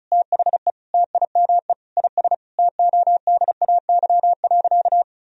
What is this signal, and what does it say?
Morse code, heard as a keyed tone, about 36 wpm, 700 Hz. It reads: THE TIME IS TODAY.
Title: Sentences from Top 1000 Words - Review Part 2 - 35wpm